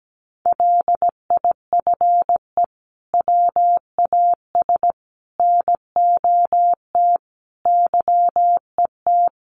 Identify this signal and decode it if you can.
Morse code, heard as a keyed tone, about 17 wpm, 700 Hz. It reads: LIFE WAS NOT YET
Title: Sentences from Top 1200 Words 17wpm